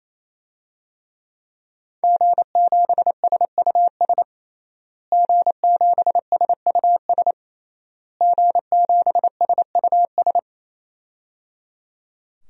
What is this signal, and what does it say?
Morse code, heard as a keyed tone, about 28 wpm, 700 Hz. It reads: G7SUS G7SUS G7SUS
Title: DXCC-100 Call Signs - 5 Characters - Repeated 3x - 28wpm